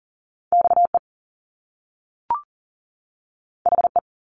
Morse code, heard as a keyed tone, about 38 wpm, 700 Hz. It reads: XE HE
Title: Top Two Letter Combinations in Words 40wpm